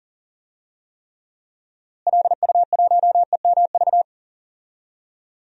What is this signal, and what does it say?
Morse code, heard as a keyed tone, about 40 wpm, 700 Hz. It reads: LU1EMV
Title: DXCC-100 Call Signs - 6 Characters - Repeated 2x - 40wpm